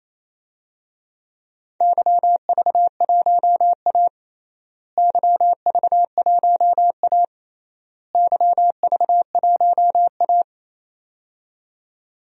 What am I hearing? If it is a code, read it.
Morse code, heard as a keyed tone, about 27 wpm, 700 Hz. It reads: YV1A YV1A YV1A
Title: DXCC-100 Call Signs - 4 Characters - Repeated 3x - 28wpm